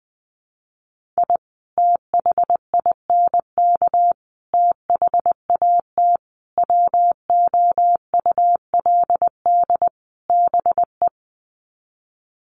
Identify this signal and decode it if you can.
Morse code, heard as a keyed tone, about 20 wpm, 700 Hz. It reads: I THINK THAT WOULD BE
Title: Sets of 5 Words Encore2 20wpm